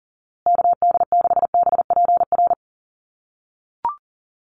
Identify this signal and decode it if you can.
Morse code, heard as a keyed tone, about 40 wpm, 700 Hz. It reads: KD6BPR
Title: USA Call Signs Repeated 2x 40wpm